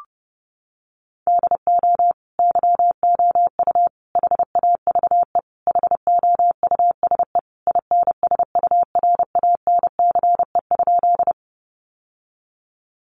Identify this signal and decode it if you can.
Morse code, heard as a keyed tone, about 30 wpm, 700 Hz. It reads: DO YOU HAVE HOUSE INSURANCE?
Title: Sentences from Top 1100 Words 30wpm